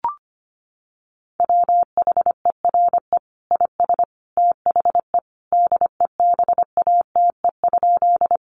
Morse code, heard as a keyed tone, about 25 wpm, 700 Hz. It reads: WHERE IS THE DEBATE?
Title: Sentences from Top 1100 Words 25wpm